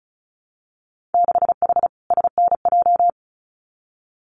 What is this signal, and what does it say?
Morse code, heard as a keyed tone, about 36 wpm, 700 Hz. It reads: 6H SNJ
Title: Winter Field Day Exchanges Encore 35wpm